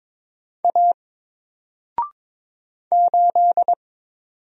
Morse code, heard as a keyed tone, about 22 wpm, 700 Hz. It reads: A 8
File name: Single-Letter-Number-22wpm